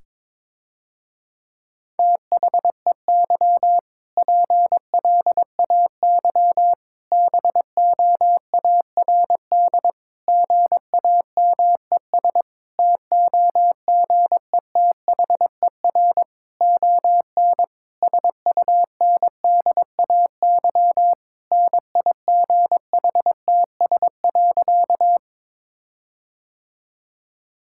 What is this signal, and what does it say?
Morse code, heard as a keyed tone, about 21 wpm, 700 Hz. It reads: THEY PLAY BOARD GAMES TOGETHER ON SUNDAY NIGHTS.
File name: Difficult Sentences with Hints - 22wpm